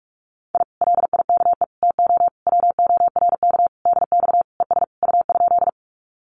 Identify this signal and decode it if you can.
Morse code, heard as a keyed tone, about 44 wpm, 700 Hz. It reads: I LIKE TO WORK DX ES U?
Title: Common QSO Phrases Encore 45wpm